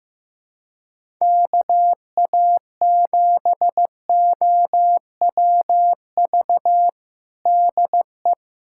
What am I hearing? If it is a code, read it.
Morse code, heard as a keyed tone, about 15 wpm, 700 Hz. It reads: KA7OWV DE
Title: USA Callsign DE Callsign - 6 Characters - 15wpm